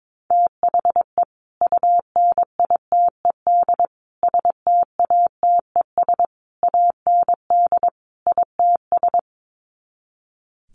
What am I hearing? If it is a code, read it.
Morse code, heard as a keyed tone, about 22 wpm, 700 Hz. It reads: THE UNITED STATES AND ITS
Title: Sets of 5 Words Encore 22wpm